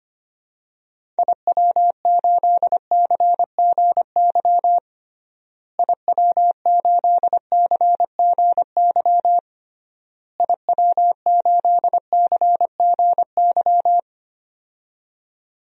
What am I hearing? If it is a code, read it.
Morse code, heard as a keyed tone, about 25 wpm, 700 Hz. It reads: IW8CGY IW8CGY IW8CGY
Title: DXCC-100 Call Signs - 6 Characters - Repeated 3x - 25wpm